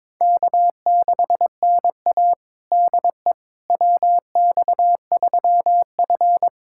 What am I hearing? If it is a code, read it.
Morse code, heard as a keyed tone, about 22 wpm, 700 Hz. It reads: K6NA DE WX3F
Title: USA Callsign DE Callsign - 4 Characters - 22wpm